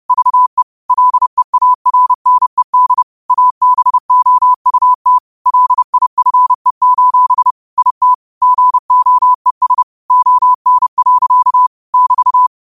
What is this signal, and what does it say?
Morse code, heard as a keyed tone, about 30 wpm, 1000 Hz. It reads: VE LEARNED ABOUT LIFE: IT GOES ON. <BT>